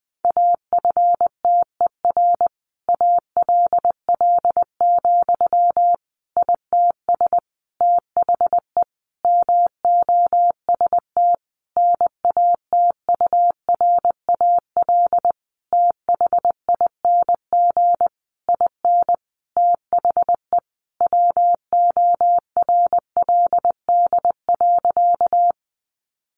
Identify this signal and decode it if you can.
Morse code, heard as a keyed tone, about 20 wpm, 700 Hz. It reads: AFTER ALL, ITS THE MOST NATURAL THING IN THE WORLD.
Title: Sentences from Top 700 Words 20wpm